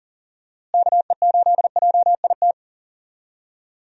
Morse code, heard as a keyed tone, about 40 wpm, 700 Hz. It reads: KE8JIT
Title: USA Call Signs - 6 Characters - Repeated 2x - 40wpm